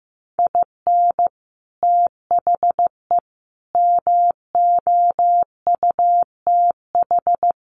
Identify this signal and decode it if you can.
Morse code, heard as a keyed tone, about 15 wpm, 700 Hz. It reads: IN THE MOUTH
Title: Sentences from Top 1500 Words 15wpm